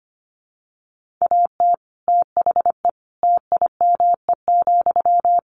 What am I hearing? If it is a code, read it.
Morse code, heard as a keyed tone, about 25 wpm, 700 Hz. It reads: AT THE TIME,
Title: Sentences from Top 1400 Words 25wpm